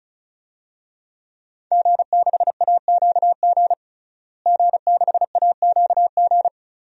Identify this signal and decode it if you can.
Morse code, heard as a keyed tone, about 36 wpm, 700 Hz. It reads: G6AQG G6AQG
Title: DXCC-100 Call Signs - 5 Characters - Repeated 3x - 35wpm